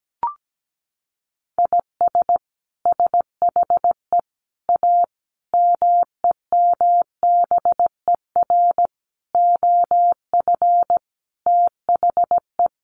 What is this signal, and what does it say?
Morse code, heard as a keyed tone, about 17 wpm, 700 Hz. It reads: IS SHE A MEMBER OF THE
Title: Sentences from Top 1100 Words 17wpm